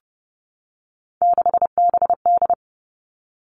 Morse code, heard as a keyed tone, about 30 wpm, 700 Hz. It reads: 6BD
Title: Three Challenging Letter-Number Combinations 30wpm